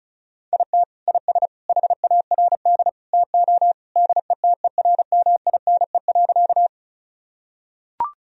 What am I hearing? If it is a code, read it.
Morse code, heard as a keyed tone, about 34 wpm, 700 Hz. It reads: IT IS HARD TO DETERMINE.
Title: Sentences from Top 1000 Words - Review Part 4 - 35wpm